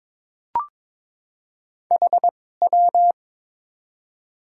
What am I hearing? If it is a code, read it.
Morse code, heard as a keyed tone, about 22 wpm, spaced as for 16 wpm, 700 Hz. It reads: HW
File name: Top 90 QSO Elements - 2x Character Spacing - 22wpm